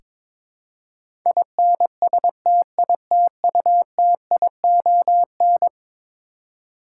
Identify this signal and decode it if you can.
Morse code, heard as a keyed tone, about 22 wpm, 700 Hz. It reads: INSTITUTION
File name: practice / 22wpm / Eleven Letter Words - 22wpm